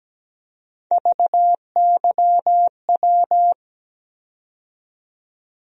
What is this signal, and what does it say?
Morse code, heard as a keyed tone, about 17 wpm, 700 Hz. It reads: VYW
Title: USA Call Sign Postfixes - 3 Letters 17wpm